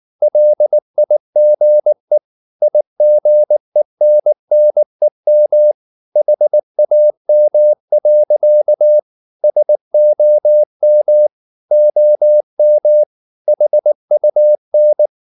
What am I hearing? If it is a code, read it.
Morse code, heard as a keyed tone, about 19 wpm, 600 Hz. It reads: LIGE IGENNEM HAM. SOM OM HUN
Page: Ferie 19 wpm